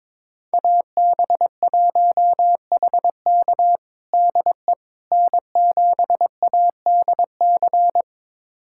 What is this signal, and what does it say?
Morse code, heard as a keyed tone, about 21 wpm, 700 Hz. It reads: AB1HK DE N7ADC
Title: USA Callsign DE Callsign - 5 Characters - 22wpm